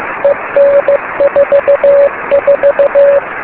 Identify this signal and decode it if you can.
Morse code, heard as a keyed tone, about 15 wpm, 560 Hz, with noise. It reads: EN44
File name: beacons / k9mu